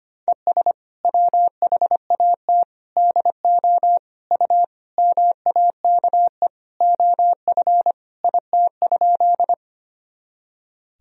Morse code, heard as a keyed tone, about 25 wpm, 700 Hz. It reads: ES WHAT DO U MAKE OF IT?
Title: Sentences from Top 100 Words and Abbreviations 25wpm